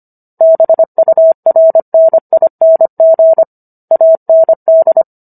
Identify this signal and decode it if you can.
Morse code, heard as a keyed tone, about 25 wpm, 650 Hz. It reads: BURNING AND